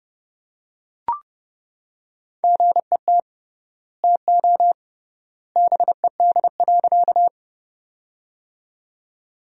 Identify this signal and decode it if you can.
Morse code, heard as a keyed tone, about 30 wpm, 700 Hz. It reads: GET TO BED.
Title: 3x Word-Spacing - Sentences from Top 500 Words - 30wpm